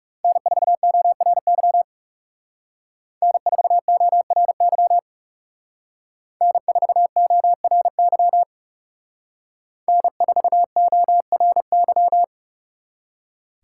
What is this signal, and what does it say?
Morse code, heard as a keyed tone, about 38 wpm, 700 Hz. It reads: N4ORY N4ORY N4ORY N4ORY